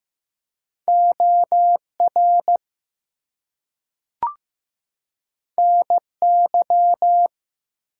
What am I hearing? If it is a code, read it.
Morse code, heard as a keyed tone, about 15 wpm, 700 Hz. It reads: OR NY
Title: US States and Canadian Provinces 15wpm